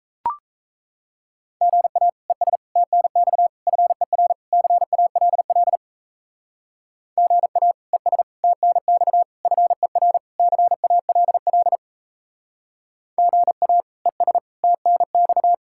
Morse code, heard as a keyed tone, about 35 wpm, 700 Hz. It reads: GA ES TNX FER CALL GA ES TNX FER CALL GA ES TNX